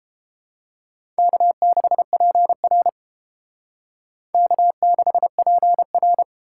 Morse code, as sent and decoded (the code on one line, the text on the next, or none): -.- -.... .--. .-. / -.- -.... .--. .-.
K6PR K6PR